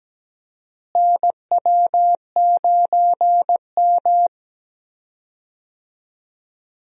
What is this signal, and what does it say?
Morse code, heard as a keyed tone, about 17 wpm, 700 Hz. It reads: NW9M
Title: USA Call Signs - 4 Characters - Repeated 3x - 17wpm